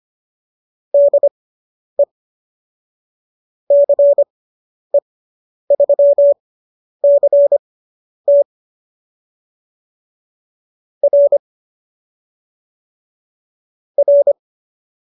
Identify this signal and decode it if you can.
Morse code, heard as a keyed tone, about 25 wpm, spaced as for 10 wpm, 570 Hz. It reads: DE CE3CT R R